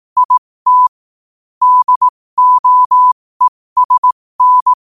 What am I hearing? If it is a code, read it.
Morse code, heard as a keyed tone, about 18 wpm, spaced as for 15 wpm, 1000 Hz. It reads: IT DOESN
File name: qotdfiles / 15wpm